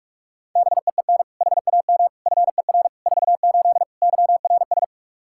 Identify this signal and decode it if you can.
Morse code, heard as a keyed tone, about 46 wpm, 700 Hz. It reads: BEEN HAM FER 48 YRS
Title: Practice QSO - Exchange 3 - More Info - 45wpm